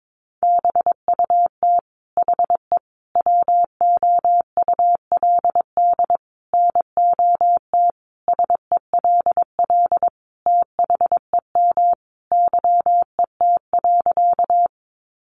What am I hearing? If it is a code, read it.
Morse code, heard as a keyed tone, about 22 wpm, 700 Hz. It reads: BUT HE WOULD NOT SELL THEM YET.
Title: Sentences from Top 1100 Words 22wpm